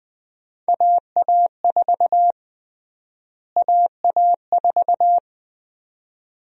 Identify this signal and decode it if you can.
Morse code, heard as a keyed tone, about 20 wpm, 700 Hz. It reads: AA4 AA4
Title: USA Call Sign Prefixes - Repeated 2x - 20wpm